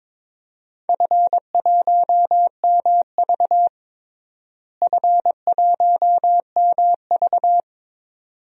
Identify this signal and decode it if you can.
Morse code, heard as a keyed tone, about 22 wpm, 700 Hz. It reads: F1MV F1MV
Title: DXCC-100 Call Signs - 4 Characters - Repeated 2x - 22wpm